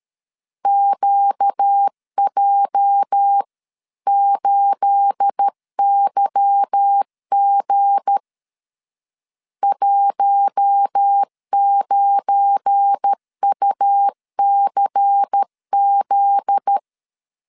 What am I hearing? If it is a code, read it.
Morse code, heard as a keyed tone, about 13 wpm, 800 Hz. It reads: QJ8YG 19UCZ
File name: CW / DET / 1 9 1500C S A 12wpm 3800